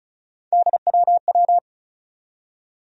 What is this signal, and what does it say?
Morse code, heard as a keyed tone, about 36 wpm, 700 Hz. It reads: DWW
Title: USA Call Sign Postfixes - 3 Letters 35wpm